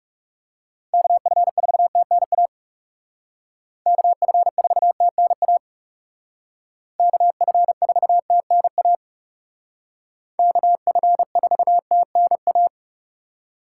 Morse code, heard as a keyed tone, about 36 wpm, 700 Hz. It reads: KF4TNA KF4TNA KF4TNA KF4TNA
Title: Speed Racing - USA 2x3 Call Signs 30wpm